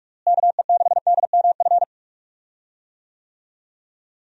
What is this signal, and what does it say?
Morse code, heard as a keyed tone, about 46 wpm, 700 Hz. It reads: KE6DMF
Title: USA Call Signs - 6 Characters - Repeated 3x - 45wpm